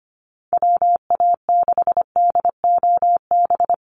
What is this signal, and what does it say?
Morse code, heard as a keyed tone, about 25 wpm, 700 Hz. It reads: WA6DOB
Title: USA Call Signs Repeated 2x 25wpm